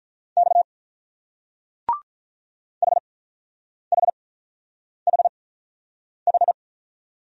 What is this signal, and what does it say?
Morse code, heard as a keyed tone, about 43 wpm, spaced as for 9 wpm, 700 Hz. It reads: X HHHH
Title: Speed-Racing Single Letters 35wpm